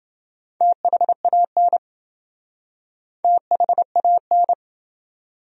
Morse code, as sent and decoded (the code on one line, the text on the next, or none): - .... .- -. / - .... .- -.
THAN THAN